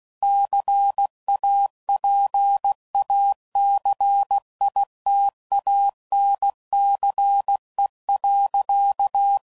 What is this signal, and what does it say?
Morse code, heard as a keyed tone, about 16 wpm, 800 Hz. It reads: CAPACITANCE.
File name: CW Code Groups @15WPM / Technical Subjects / XC